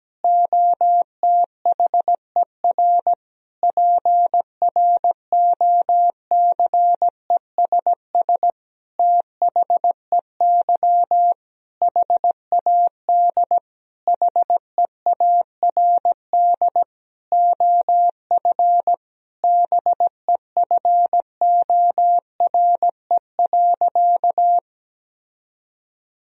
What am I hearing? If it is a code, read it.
Morse code, heard as a keyed tone, about 17 wpm, 700 Hz. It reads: OTHER PROCESS THEY HAD HEARD OF BEFORE.
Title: Sentences from Top 1000 Words - Review Part 3 - 17wpm